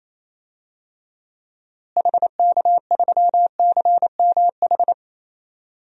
Morse code, heard as a keyed tone, about 28 wpm, 700 Hz. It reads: HK3CMH
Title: DXCC-100 Call Signs - 6 Characters - Repeated 3x - 28wpm